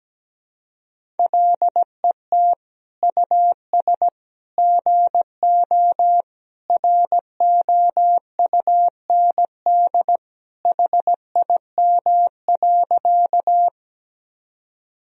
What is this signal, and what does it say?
Morse code, heard as a keyed tone, about 17 wpm, 700 Hz. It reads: LET US GO ROUND HIM.
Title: Sentences from Top 1000 Words - Review Part 1 - 17wpm